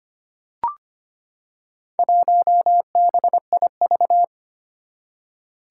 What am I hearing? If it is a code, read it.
Morse code, heard as a keyed tone, about 25 wpm, 700 Hz. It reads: 1BIV
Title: Four Challenging Letter-Number Combinations 25wpm